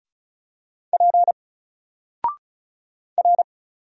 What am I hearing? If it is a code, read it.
Morse code, heard as a keyed tone, about 36 wpm, 700 Hz. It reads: P R
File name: Single-Letter-Number-35wpm